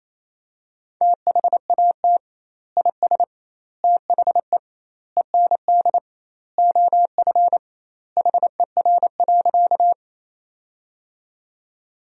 Morse code, heard as a keyed tone, about 27 wpm, 700 Hz. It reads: THAT IS THE END OF HER.
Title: 2x Word-Spacing - Sentences from Top 200 Words - 28wpm